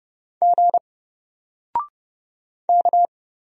Morse code, heard as a keyed tone, about 30 wpm, 700 Hz. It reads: G K
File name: Single-Letter-Number-30wpm